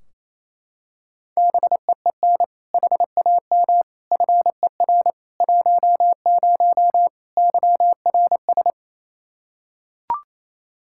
Morse code, heard as a keyed tone, about 28 wpm, 700 Hz. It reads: BEEN HAM FER 10 YRS
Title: Practice QSO - Exchange 3 - More Info - 28wpm